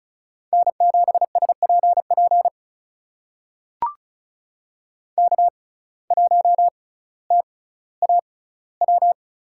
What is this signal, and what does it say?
Morse code, heard as a keyed tone, about 36 wpm, spaced as for 18 wpm, 700 Hz. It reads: N7SPP K1TAW
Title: USA Call Signs - 5 Characters - 6x Character Spacing - 35wpm